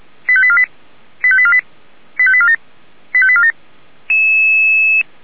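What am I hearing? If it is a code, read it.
Five-tone selective call (ZVEI), heard as quick runs of five tones, tones of about 70 ms. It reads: 85848 85848 85847 85847
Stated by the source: ZVEI five-tone selective call received by SDR.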